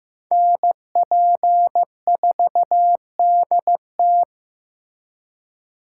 Morse code, heard as a keyed tone, about 18 wpm, 700 Hz. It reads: NP4DT
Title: Speed Racing - USA 2x2 Call Signs 15wpm